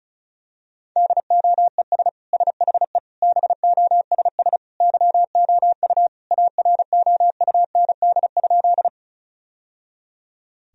Morse code, heard as a keyed tone, about 36 wpm, 700 Hz. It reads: DOES SHE BOSS YOU AROUND?
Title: Sentences from Top 1000 Words - Review Part 4 - 35wpm